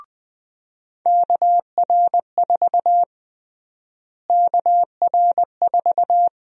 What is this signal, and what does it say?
Morse code, heard as a keyed tone, about 21 wpm, 700 Hz. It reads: KR4 KR4
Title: USA Call Sign Prefixes - Repeated 2x - 20wpm